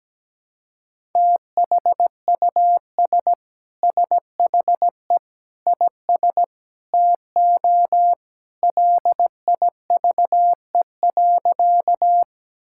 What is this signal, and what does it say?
Morse code, heard as a keyed tone, about 17 wpm, 700 Hz. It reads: THUS SHE IS TO LIVE.
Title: Sentences from Top 1000 Words - Review Part 5 - 17wpm